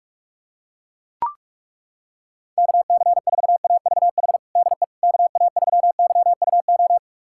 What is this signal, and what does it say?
Morse code, heard as a keyed tone, about 42 wpm, 700 Hz. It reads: KC4AVH DE KA3YUO
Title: USA Callsign DE Callsign - 6 Characters - 45wpm